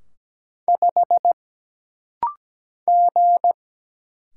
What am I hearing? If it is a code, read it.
Morse code, heard as a keyed tone, about 17 wpm, 700 Hz. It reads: 5 G
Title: Single Letter-Number Alt - Rapid-Fire 17wpm